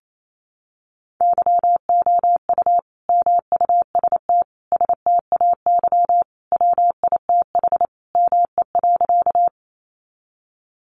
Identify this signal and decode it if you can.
Morse code, heard as a keyed tone, about 28 wpm, 700 Hz. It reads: YOU MUST STAY WITH ME.
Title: Sentences from Top 500 Words 28wpm